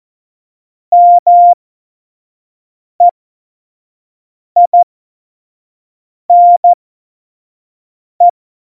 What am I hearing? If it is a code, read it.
Morse code, heard as a keyed tone, about 14 wpm, spaced as for 5 wpm, 700 Hz. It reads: MEINE
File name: Morsekurs / Witze / Witz 04 August 2020, 25 F70